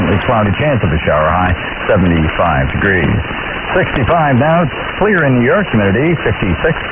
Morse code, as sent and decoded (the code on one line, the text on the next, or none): none